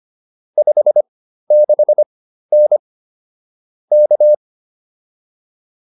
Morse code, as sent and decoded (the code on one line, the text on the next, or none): ..... -.... -. / -.-
56N K